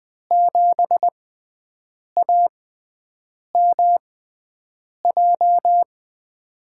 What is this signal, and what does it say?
Morse code, heard as a keyed tone, about 20 wpm, spaced as for 7 wpm, 700 Hz. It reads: 7AMJ
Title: USA Call Signs - 5 Characters - 6x Character Spacing - 20wpm